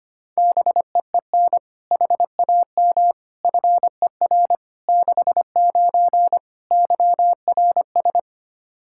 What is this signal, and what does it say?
Morse code, heard as a keyed tone, about 25 wpm, 700 Hz. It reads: BEEN HAM FER 69 YRS
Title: Practice QSO - Exchange 3 - More Info - 25wpm